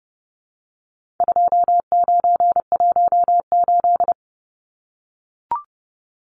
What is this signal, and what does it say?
Morse code, heard as a keyed tone, about 30 wpm, 700 Hz. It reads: 2918
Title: Numbers - 4-Digit 30wpm